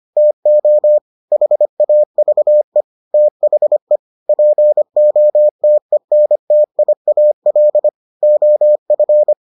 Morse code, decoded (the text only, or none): TO HAVE THE POTENTIAL OF